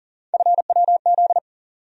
Morse code, heard as a keyed tone, about 40 wpm, 700 Hz. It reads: FWZ
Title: USA Call Sign Postfixes - 3 Letters 40wpm